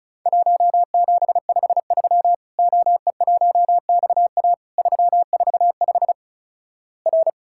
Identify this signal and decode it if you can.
Morse code, heard as a keyed tone, about 36 wpm, 700 Hz. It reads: 1753 OE1XA 345 R
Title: Worked All Europe DX Contest - QTC Exchange - 35wpm